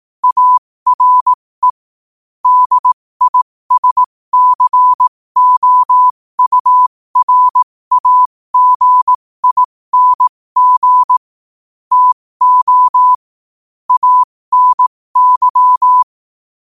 Morse code, heard as a keyed tone, about 18 wpm, spaced as for 15 wpm, 1000 Hz. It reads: ARE DISCOURAGING TO ANY
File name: qotdfiles / 15wpm